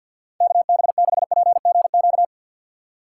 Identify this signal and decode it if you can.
Morse code, heard as a keyed tone, about 50 wpm, 700 Hz. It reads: KB6PGQ
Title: USA Call Signs - 3x Character Spacing - 50wpm